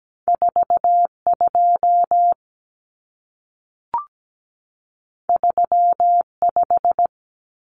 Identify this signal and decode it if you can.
Morse code, heard as a keyed tone, about 17 wpm, 700 Hz. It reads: 42 35
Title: Numbers - 2-Digit 17wpm